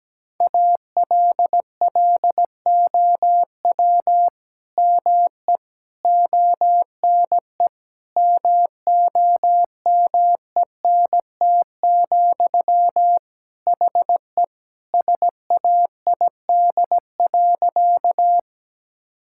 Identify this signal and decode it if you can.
Morse code, heard as a keyed tone, about 17 wpm, 700 Hz. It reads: ALLOW ME ONE MOMENT, HE SAID.
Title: Sentences from Top 1000 Words - Review Part 5 - 17wpm